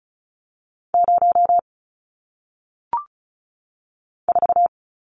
Morse code, heard as a keyed tone, about 36 wpm, 700 Hz. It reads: 0 4
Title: Numbers - 1-Digit 35wpm